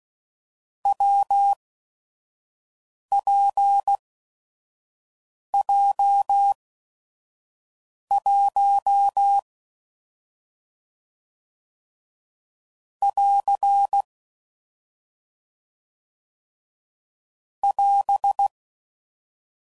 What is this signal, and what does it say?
Morse code, heard as a keyed tone, about 16 wpm, spaced as for 5 wpm, 800 Hz. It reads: WPJ1 <AR> <AS>
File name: Combinations / Difficult Combos / WPJ WPJ1